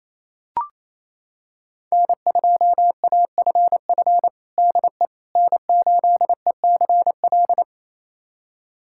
Morse code, heard as a keyed tone, about 28 wpm, 700 Hz. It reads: N2AFF DE N8ECL
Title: USA Callsign DE Callsign - 5 Characters - 28wpm